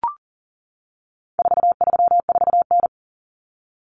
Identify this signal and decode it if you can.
Morse code, heard as a keyed tone, about 40 wpm, 700 Hz. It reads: 434N
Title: Common Cut Numbers - 4-Digit 40wpm